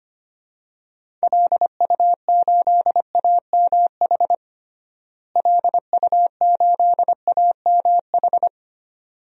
Morse code, heard as a keyed tone, about 25 wpm, 700 Hz. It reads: LU8AMH LU8AMH
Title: DXCC-100 Call Signs - 6 Characters - Repeated 2x - 25wpm